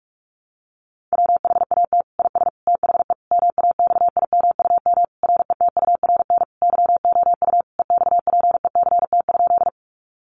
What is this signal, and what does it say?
Morse code, heard as a keyed tone, about 44 wpm, 700 Hz. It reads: WHAT IS THE MAXIMUM RETURN YOU EXPECT?
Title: Sentences from Top 1300 Words 45wpm